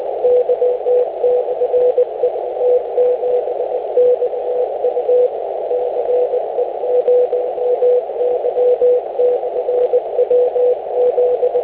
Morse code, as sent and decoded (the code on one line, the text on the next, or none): none